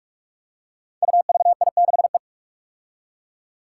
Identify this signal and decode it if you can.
Morse code, heard as a keyed tone, about 44 wpm, 700 Hz. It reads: UVI6E
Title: Code Groups - 5 Letter:Numbers - 45wpm